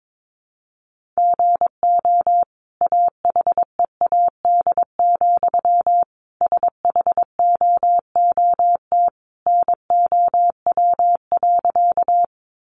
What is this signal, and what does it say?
Morse code, heard as a keyed tone, about 22 wpm, 700 Hz. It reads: GO AHEAD, SHOOT NOW.
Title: Sentences from Top 1200 Words 22wpm